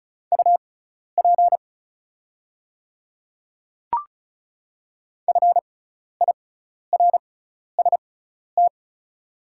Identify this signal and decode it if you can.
Morse code, heard as a keyed tone, about 36 wpm, spaced as for 12 wpm, 700 Hz. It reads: UP FIRST
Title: 6x Character Spacing - Top 100 Words - 35wpm